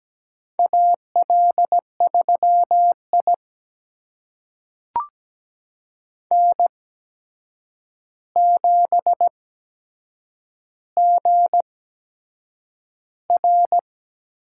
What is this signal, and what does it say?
Morse code, heard as a keyed tone, about 17 wpm, spaced as for 7 wpm, 700 Hz. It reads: AL3I N7GR
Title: USA Call Signs - 4 Characters - 8x Character Spacing - 17wpm